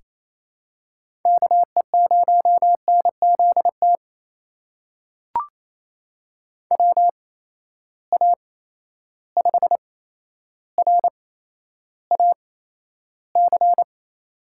Morse code, heard as a keyed tone, about 28 wpm, spaced as for 12 wpm, 700 Hz. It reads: KE0NZT WA5RAC
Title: USA Call Signs - 6 Characters - 8x Character Spacing - 28wpm